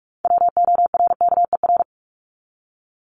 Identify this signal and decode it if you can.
Morse code, heard as a keyed tone, about 44 wpm, 700 Hz. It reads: WORKER
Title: ES:S Suffix - Top 100 Words with Base Words 45wpm